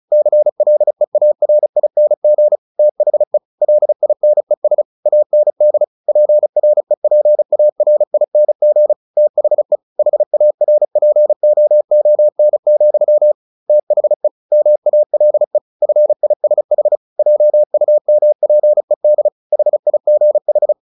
Morse code, heard as a keyed tone, about 34 wpm, 600 Hz. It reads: CLEARING THE LINES AND PREPARING THE HARPOON, THE MALE FISH JUMPED HIGH
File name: the old man and the sea / the old man and the sea 1 35wpm